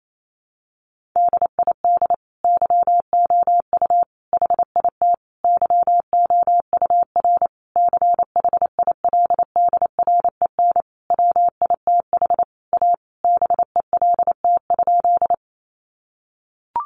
Morse code, heard as a keyed tone, about 28 wpm, 700 Hz. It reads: DID YOU HIT YOUR CHILDREN WITH A BELT?
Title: Sentences from Top 1400 Words 28wpm